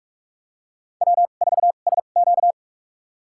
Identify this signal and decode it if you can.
Morse code, heard as a keyed tone, about 46 wpm, spaced as for 32 wpm, 690 Hz. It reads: W4SQ
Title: USA Call Signs - 4 Characters - 2x Character Spacing - 45wpm